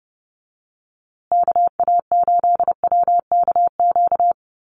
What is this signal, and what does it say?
Morse code, heard as a keyed tone, about 30 wpm, 700 Hz. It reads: KA8WKQ
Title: USA Call Signs Repeated 2x 30wpm